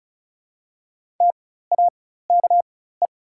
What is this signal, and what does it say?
Morse code, heard as a keyed tone, about 34 wpm, spaced as for 16 wpm, 700 Hz. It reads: TAKE
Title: 4x Character Spacing - Top 200 Words - 35wpm